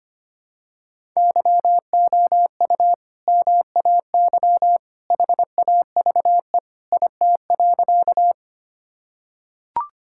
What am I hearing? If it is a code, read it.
Morse code, heard as a keyed tone, about 25 wpm, 700 Hz. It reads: YOU MAY HAVE IT.
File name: practice / 25wpm / Sentences with 2-Word Phrases - 25wpm